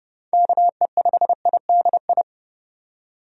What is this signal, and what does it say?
Morse code, heard as a keyed tone, about 30 wpm, 700 Hz. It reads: KE5IDI
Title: USA Call Signs - 6 Characters - 30wpm